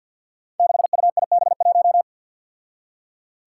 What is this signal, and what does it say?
Morse code, heard as a keyed tone, about 47 wpm, 700 Hz. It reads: 6UIB1
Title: Five Challenging Letter-Number Combinations 50wpm